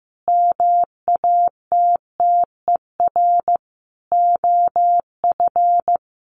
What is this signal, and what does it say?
Morse code, heard as a keyed tone, about 14 wpm, 700 Hz. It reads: MATTER OF
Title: Sentences from Top 1500 Words 15wpm